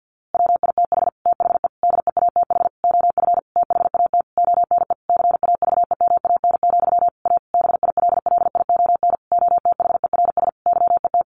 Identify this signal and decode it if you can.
Morse code, heard as a keyed tone, about 50 wpm, 700 Hz. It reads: WITH THE DEATH OF THAT ONE CAVEMAN, A BILLION OTHERS YET
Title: A Sound Of Thunder 50wpm